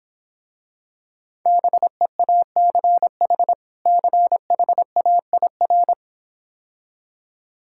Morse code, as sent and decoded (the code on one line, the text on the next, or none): -... .- -.-. .... / -.-. .... .- .. .-.
BEACH CHAIR